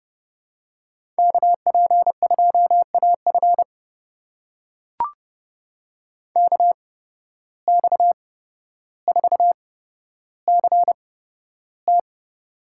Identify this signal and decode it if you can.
Morse code, heard as a keyed tone, about 31 wpm, spaced as for 13 wpm, 700 Hz. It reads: KP2AF KX4CT
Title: USA Call Signs - 5 Characters - 8x Character Spacing - 30wpm